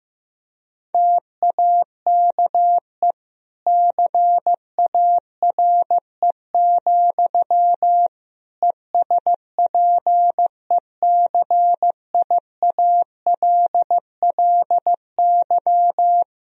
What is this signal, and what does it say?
Morse code, heard as a keyed tone, about 15 wpm, 700 Hz. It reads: TAKE CARE, ESPECIALLY
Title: Sentences from Top 1000 Words - Review Part 4 - 15wpm